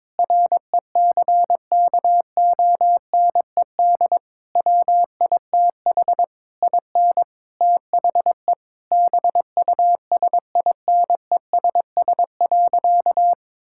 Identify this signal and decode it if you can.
Morse code, heard as a keyed tone, about 22 wpm, 700 Hz. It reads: RECKONED WITH IN THE BUSINESS.
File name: practice / 22wpm / Sentences with 8-Word Phrases - 22wpm